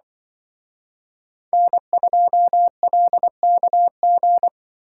Morse code, heard as a keyed tone, about 25 wpm, 700 Hz. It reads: N2LKG
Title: Speed Racing - USA 1x3 Call Signs 20wpm